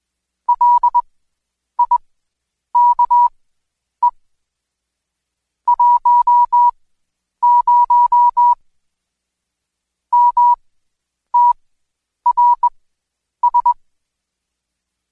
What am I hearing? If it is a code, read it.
Morse code, heard as a keyed tone, about 20 wpm, spaced as for 9 wpm, 970 Hz. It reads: LIKE 10 MTRS
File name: Code Quick / AudioDownld2 / Seg116-Day21-DayDrill 21